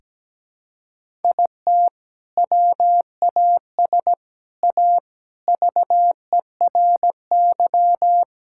Morse code, heard as a keyed tone, about 17 wpm, 700 Hz. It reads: IT WAS A VERY
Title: Sentences from Top 1000 Words - Review Part 5 - 17wpm